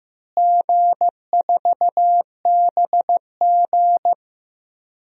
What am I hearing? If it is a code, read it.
Morse code, heard as a keyed tone, about 15 wpm, 700 Hz. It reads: G4BG
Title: DXCC-100 Call Signs - 4 Characters - Repeated 2x - 15wpm